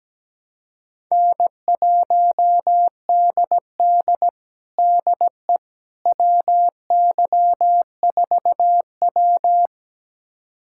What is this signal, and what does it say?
Morse code, heard as a keyed tone, about 17 wpm, 700 Hz. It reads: N1DD DE WY4W
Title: USA Callsign DE Callsign - 4 Characters - 17wpm